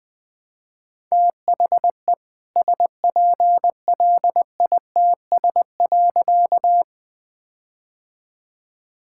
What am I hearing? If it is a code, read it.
Morse code, heard as a keyed tone, about 20 wpm, 700 Hz. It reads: THE SPLITS.
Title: All-Age Appropriate Jokes - 20wpm